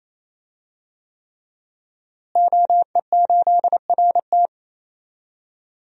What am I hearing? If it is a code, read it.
Morse code, heard as a keyed tone, about 28 wpm, 700 Hz. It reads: OE8RT
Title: DXCC-100 Call Signs - 5 Characters - Repeated 3x - 28wpm